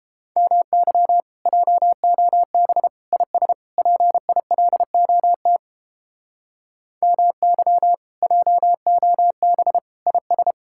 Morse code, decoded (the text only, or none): MY JOB IS PILOT MY JOB IS